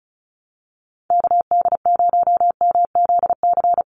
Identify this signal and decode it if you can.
Morse code, heard as a keyed tone, about 34 wpm, 700 Hz. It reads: KD0MZC
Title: USA Call Signs 35wpm